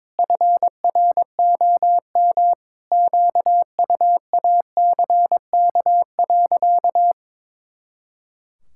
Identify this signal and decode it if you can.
Morse code, heard as a keyed tone, about 22 wpm, 700 Hz. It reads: FROM QUACK.
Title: Pangram Challenge v2 - 22wpm